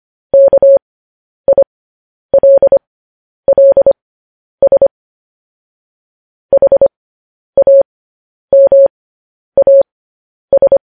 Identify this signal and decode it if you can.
Morse code, heard as a keyed tone, about 25 wpm, spaced as for 10 wpm, 550 Hz. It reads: KILLS HAMAS
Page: news-headlines-in-morse-code-at-10-wpm